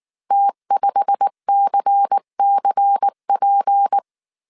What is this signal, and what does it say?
Morse code, heard as a keyed tone, about 19 wpm, 800 Hz. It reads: T5CCP
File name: CW / DET / 1 9 1500C S B 18wpm 2551